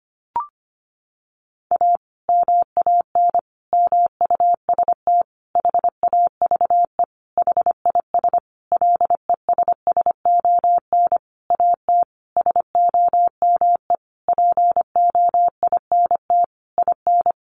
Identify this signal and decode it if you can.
Morse code, heard as a keyed tone, about 25 wpm, 700 Hz. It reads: A MAN MUST HAVE HIS LESSON AT SOME POINT IN
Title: Sentences from Top 1200 Words 25wpm